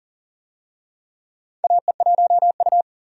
Morse code, heard as a keyed tone, about 40 wpm, 700 Hz. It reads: AE1U
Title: USA Call Signs - 4 Characters - 40wpm